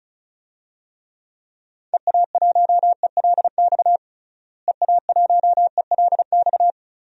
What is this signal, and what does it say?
Morse code, heard as a keyed tone, about 34 wpm, 700 Hz. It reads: EA1ELX EA1ELX
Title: DXCC-100 Call Signs - 6 Characters - Repeated 2x - 35wpm